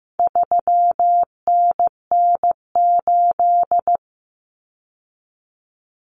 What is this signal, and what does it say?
Morse code, heard as a keyed tone, about 15 wpm, 700 Hz. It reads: 3NN8
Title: Common Cut Numbers - 4-Digit 15wpm